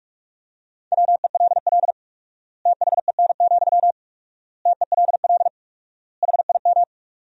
Morse code, heard as a keyed tone, about 47 wpm, 700 Hz. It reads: WELL THEN, TELL HIM
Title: 4x Word-Spacing - Sentences from Top 200 Words - 45wpm